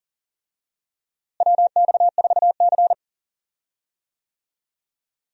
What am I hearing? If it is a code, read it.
Morse code, heard as a keyed tone, about 40 wpm, 700 Hz. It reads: WX4C
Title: USA Call Signs - 4 Characters - Repeated 2x - 40wpm